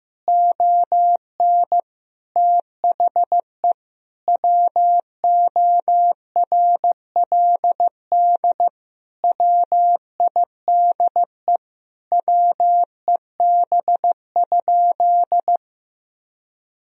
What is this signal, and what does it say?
Morse code, heard as a keyed tone, about 15 wpm, 700 Hz. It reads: ON THE WORLD WIDE WEB?
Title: Sentences from Top 1000 Words - Review Part 4 - 15wpm